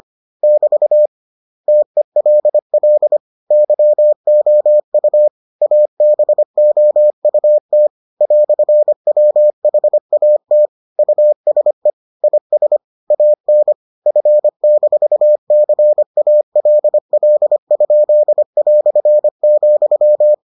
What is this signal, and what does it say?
Morse code, heard as a keyed tone, about 25 wpm, 600 Hz. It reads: <BT> TELL YOU ABOUT "WHAT USE IS AN F-CALL?",